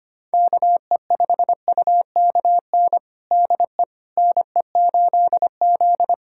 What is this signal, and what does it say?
Morse code, heard as a keyed tone, about 25 wpm, 700 Hz. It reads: KE5UKN DE NE8Z
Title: USA Callsign DE Callsign 25wpm